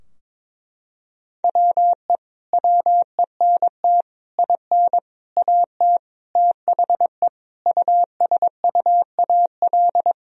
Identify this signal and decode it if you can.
Morse code, heard as a keyed tone, about 22 wpm, 700 Hz. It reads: WE WENT IN AT THE USUAL
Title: Sentences from Top 1000 Words - Review Part 2 - 22wpm